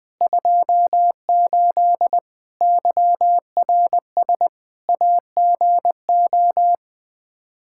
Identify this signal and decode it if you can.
Morse code, heard as a keyed tone, about 20 wpm, 700 Hz. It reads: 28 YRS AGO
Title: K1USN SST QSO Practice 20wpm